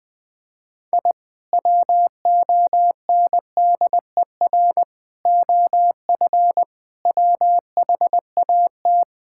Morse code, heard as a keyed tone, about 21 wpm, 700 Hz. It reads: I WONDER OF WHAT
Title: Sentences with 2-Word Phrases - 20wpm